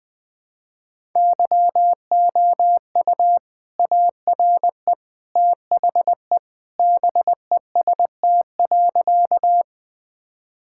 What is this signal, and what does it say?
Morse code, heard as a keyed tone, about 20 wpm, 700 Hz. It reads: YOU ARE THE BEST.
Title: Sentences with 2-Word Phrases - 20wpm